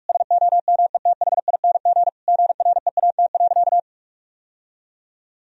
Morse code, heard as a keyed tone, about 46 wpm, 700 Hz. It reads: SOMETHING GREAT.
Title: Sentences from Top 1000 Words - Review Part 2 - 45wpm